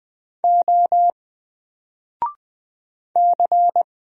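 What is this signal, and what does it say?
Morse code, heard as a keyed tone, about 20 wpm, 700 Hz. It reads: O C
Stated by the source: Single letter and number Morse drill at 20 wpm, from N4LCD.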